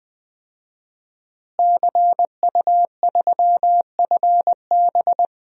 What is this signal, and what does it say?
Morse code, heard as a keyed tone, about 21 wpm, 700 Hz. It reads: CU3FB
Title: DXCC-100 Call Signs - 5 Characters - Repeated 2x - 20wpm